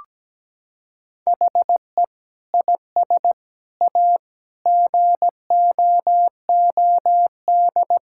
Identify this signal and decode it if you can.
Morse code, heard as a keyed tone, about 17 wpm, 700 Hz. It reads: HE IS A GOOD
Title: Sentences from Top 1600 Words 17wpm